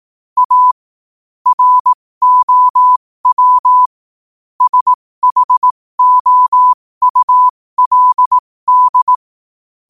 Morse code, heard as a keyed tone, about 18 wpm, spaced as for 15 wpm, 1000 Hz. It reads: A ROW SHOULD